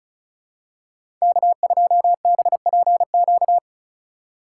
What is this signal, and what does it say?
Morse code, heard as a keyed tone, about 34 wpm, 700 Hz. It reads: K2BPQ
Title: USA Call Signs - 5 Characters - 4x Character Spacing - 35wpm